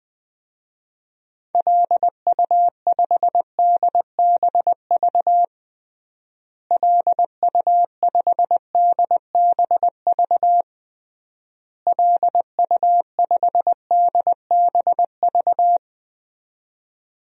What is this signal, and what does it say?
Morse code, heard as a keyed tone, about 20 wpm, 700 Hz. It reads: LU5DBV LU5DBV LU5DBV
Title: DXCC-100 Call Signs - 6 Characters - Repeated 3x - 20wpm